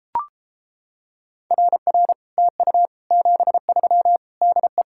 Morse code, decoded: RR TU 73 DE

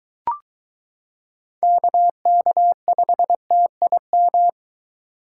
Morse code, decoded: KK5TIM

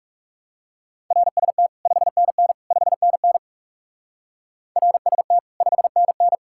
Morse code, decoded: RST 5NN 5NN RST 5NN

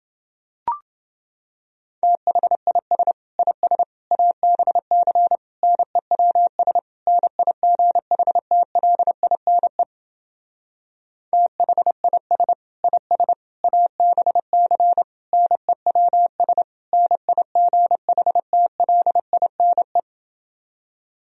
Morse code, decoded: THIS IS ABC NEWS NIGHTLINE THIS IS ABC NEWS NIGHTLINE

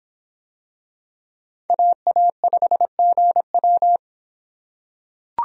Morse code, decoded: AA5GW

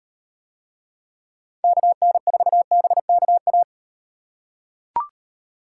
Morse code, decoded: KN4BKA

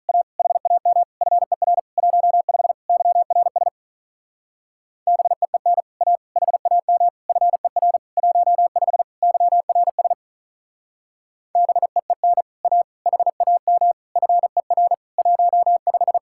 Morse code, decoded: A HAM FER 15 YRS BEEN A HAM FER 15 YRS BEEN A HAM FER 15

